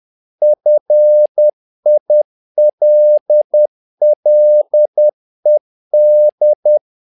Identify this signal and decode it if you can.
Morse code, heard as a keyed tone, about 10 wpm, 600 Hz. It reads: FILLED
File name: learning center / tower trouble / tower trouble 10wpm